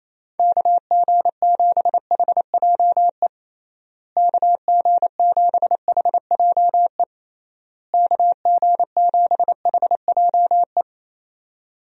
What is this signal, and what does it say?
Morse code, heard as a keyed tone, about 28 wpm, 700 Hz. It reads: KG7HJE KG7HJE KG7HJE
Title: USA Call Signs - 6 Characters - Repeated 3x - 28wpm